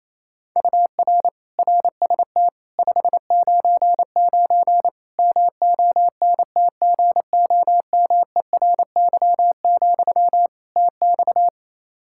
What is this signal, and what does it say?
Morse code, heard as a keyed tone, about 28 wpm, 700 Hz. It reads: UR RST 599 MONTGOMERY, TX
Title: K1USN SST QSO Practice 28wpm